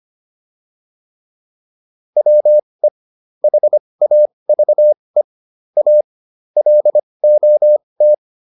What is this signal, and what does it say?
Morse code, heard as a keyed tone, about 25 wpm, spaced as for 20 wpm, 600 Hz. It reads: WE HAVE A LOT